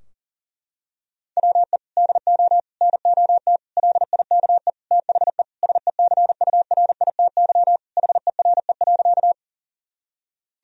Morse code, heard as a keyed tone, about 40 wpm, 700 Hz. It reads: WE DO NOT LIKE THE SECURITY HERE.
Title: Sentences from Top 1000 Words - Review Part 4 - 40wpm